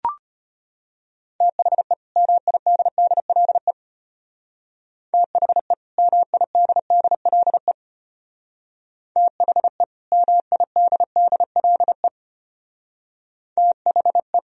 Morse code, heard as a keyed tone, about 30 wpm, 700 Hz. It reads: THE MIDDLE THE MIDDLE THE MIDDLE THE
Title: Speed-Racing - Sets of 2 Words 25wpm